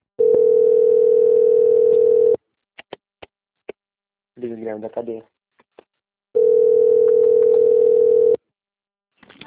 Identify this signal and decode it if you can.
Ringback tone (North America), 440 + 480 Hz, with speech.
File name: out- - - - - .24363